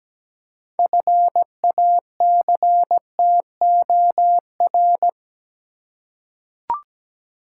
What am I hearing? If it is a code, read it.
Morse code, heard as a keyed tone, about 17 wpm, 700 Hz. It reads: FACTOR
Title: Six Letter Words - 17wpm